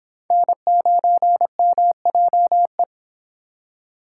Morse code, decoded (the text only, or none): N9MJE